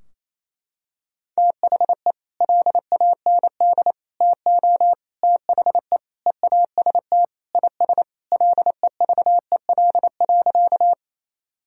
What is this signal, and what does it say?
Morse code, heard as a keyed tone, about 28 wpm, 700 Hz. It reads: THE LAND TO THE EAST IS LEVEL.
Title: Sentences from Top 1000 Words - Review Part 4 - 28wpm